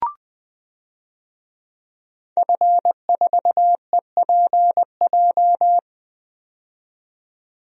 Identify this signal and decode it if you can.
Morse code, heard as a keyed tone, about 20 wpm, 700 Hz. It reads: F4EPJ
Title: DXCC-100 Call Signs - 5 Characters - 20wpm